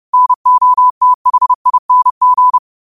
Morse code, heard as a keyed tone, about 30 wpm, 1000 Hz. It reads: NOTHING